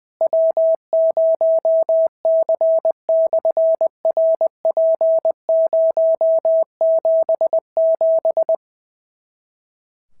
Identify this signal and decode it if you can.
Morse code, heard as a keyed tone, about 20 wpm, 650 Hz. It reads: W0C/RP077
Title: SOTA QSO Practice 20wpm